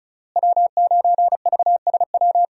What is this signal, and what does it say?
Morse code, heard as a keyed tone, about 34 wpm, 700 Hz. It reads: W9VSW
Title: USA Call Signs - 5 Characters - Repeated 3x - 35wpm